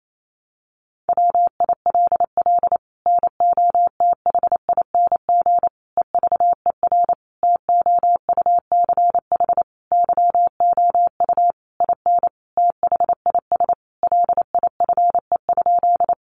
Morse code, heard as a keyed tone, about 28 wpm, 700 Hz. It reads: WILL NOTHING EVER TOUCH YOU IN THIS LIFE?
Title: Sentences from Top 1200 Words 28wpm